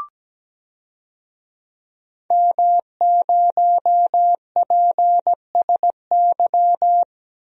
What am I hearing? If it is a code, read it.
Morse code, heard as a keyed tone, about 17 wpm, 700 Hz. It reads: M0PSY
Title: DXCC-100 Call Signs - 5 Characters - Repeated 2x - 17wpm